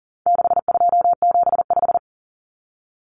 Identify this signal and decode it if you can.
Morse code, heard as a keyed tone, about 39 wpm, 700 Hz. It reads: 6275
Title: Numbers - 4-Digit 40wpm